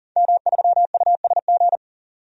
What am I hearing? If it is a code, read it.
Morse code, heard as a keyed tone, about 37 wpm, 700 Hz. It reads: M3USG
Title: DXCC-100 Call Signs - 5 Characters - 40wpm